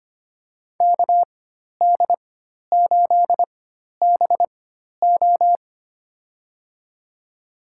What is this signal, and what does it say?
Morse code, heard as a keyed tone, about 25 wpm, spaced as for 12 wpm, 700 Hz. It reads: KD8BO
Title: USA Call Signs - 5 Characters - 4x Character Spacing - 25wpm